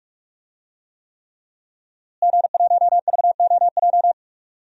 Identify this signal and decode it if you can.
Morse code, heard as a keyed tone, about 46 wpm, 700 Hz. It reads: G1VOJ
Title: DXCC-100 Call Signs - 5 Characters - Repeated 3x - 45wpm